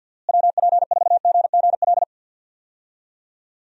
Morse code, heard as a keyed tone, about 47 wpm, 700 Hz. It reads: WP4GGL
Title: USA Call Signs - 6 Characters - Repeated 3x - 50wpm